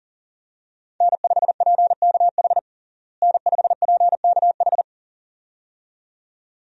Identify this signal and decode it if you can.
Morse code, heard as a keyed tone, about 37 wpm, 700 Hz. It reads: N5PKH N5PKH
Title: USA Call Signs - 5 Characters - Repeated 2x - 40wpm